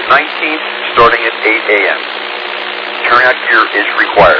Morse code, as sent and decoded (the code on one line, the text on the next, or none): none